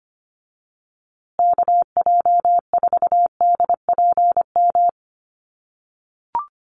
Morse code, heard as a keyed tone, about 25 wpm, 700 Hz. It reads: KJ4DPM
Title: USA Call Signs Repeated 2x 25wpm